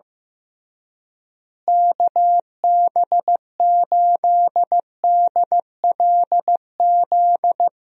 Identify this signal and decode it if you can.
Morse code, heard as a keyed tone, about 15 wpm, 700 Hz. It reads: KB8DLZ
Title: USA Call Signs - 6 Characters - Repeated 3x - 15wpm